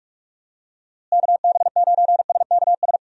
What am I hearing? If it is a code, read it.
Morse code, heard as a keyed tone, about 47 wpm, 700 Hz. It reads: KB9SKS
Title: USA Call Signs - 6 Characters - Repeated 2x - 45wpm